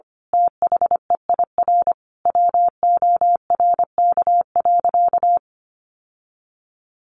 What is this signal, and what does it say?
Morse code, heard as a keyed tone, about 25 wpm, 700 Hz. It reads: THEIR WORK.
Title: Sentences from Top 1100 Words 25wpm